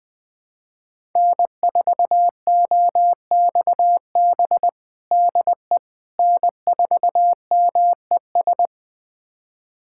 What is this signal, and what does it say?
Morse code, heard as a keyed tone, about 21 wpm, 700 Hz. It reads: N4OXB DE N4MES
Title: USA Callsign DE Callsign - 5 Characters - 20wpm